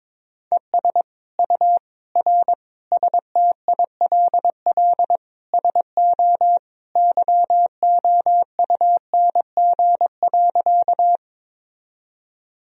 Morse code, heard as a keyed tone, about 22 wpm, 700 Hz. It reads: ES U R STILL SO YOUNG.
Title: Sentences from Top 400 Words and Abbreviations 22wpm